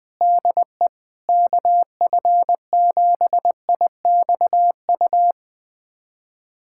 Morse code, heard as a keyed tone, about 20 wpm, 700 Hz. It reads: DE KF7IXU
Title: USA Callsign DE Callsign - 6 Characters - 20wpm